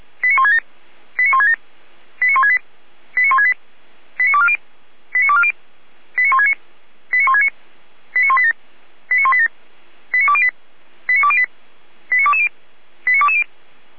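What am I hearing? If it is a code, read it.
Five-tone selective call (ZVEI), heard as quick runs of five tones, tones of about 70 ms. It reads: 89167 89167 89168 89168 89240 89240 89169 89169 89187 89187 89208 89208 89220 89220
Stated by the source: ZVEI five-tone selective call received on SDR.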